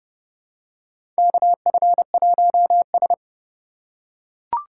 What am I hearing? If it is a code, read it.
Morse code, heard as a keyed tone, about 32 wpm, 700 Hz. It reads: KF1S
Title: USA Call Signs - 4 Characters - 4x Character Spacing - 30wpm